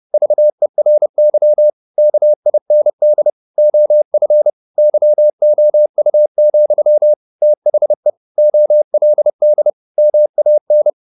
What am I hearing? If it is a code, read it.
Morse code, heard as a keyed tone, about 30 wpm, 600 Hz. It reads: VERY KIND OF YOU, THE OLD MAN
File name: the old man and the sea / the old man and the sea 1 30wpm